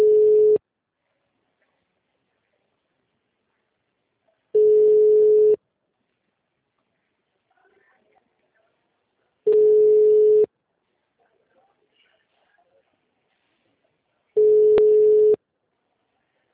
Ringback tone (Europe, 425 Hz).